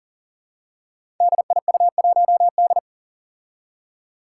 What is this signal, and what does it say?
Morse code, heard as a keyed tone, about 40 wpm, 700 Hz. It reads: DIU1D
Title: Five Challenging Letter-Number Combinations 40wpm